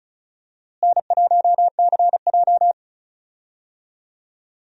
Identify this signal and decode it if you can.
Morse code, heard as a keyed tone, about 34 wpm, 700 Hz. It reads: N1CJ
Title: USA Call Signs - 4 Characters - Repeated 2x - 35wpm